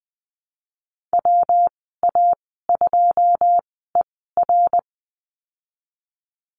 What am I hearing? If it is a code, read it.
Morse code, heard as a keyed tone, about 19 wpm, spaced as for 14 wpm, 700 Hz. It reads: WA2ER
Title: 128 Lesson 21g - Number 6 - Call Signs 20wpm